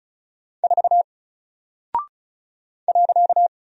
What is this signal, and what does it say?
Morse code, heard as a keyed tone, about 36 wpm, 700 Hz. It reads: 4 .
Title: Single Character (Letter:Number:Punctuation:Prosign) Alt 35wpm